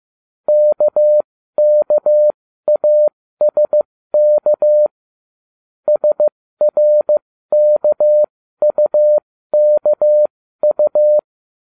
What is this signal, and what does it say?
Morse code, heard as a keyed tone, about 15 wpm, spaced as for 13 wpm, 600 Hz. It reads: KKASK SRKUKU